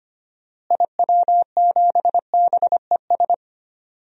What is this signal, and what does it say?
Morse code, heard as a keyed tone, about 25 wpm, 700 Hz. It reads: IW7BES
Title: DXCC-100 Call Signs - 6 Characters - Repeated 3x - 25wpm